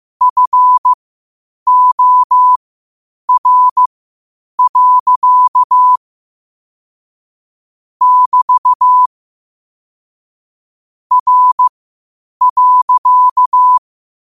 Morse code, heard as a keyed tone, about 15 wpm, spaced as for 8 wpm, 1000 Hz. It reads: FOR. <BT> R.